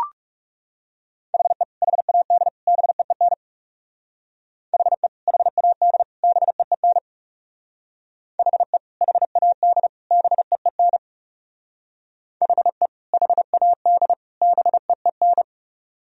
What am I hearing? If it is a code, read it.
Morse code, heard as a keyed tone, about 37 wpm, 700 Hz. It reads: HE HAD BEEN HE HAD BEEN HE HAD BEEN HE HAD BEEN
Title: Speed-Racing - Sets of 3 Words 30wpm